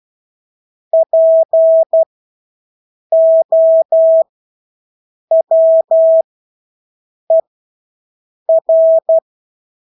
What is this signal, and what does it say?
Morse code, heard as a keyed tone, about 12 wpm, spaced as for 6 wpm, 650 Hz. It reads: POWER
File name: Engelske ord 12-6 wpm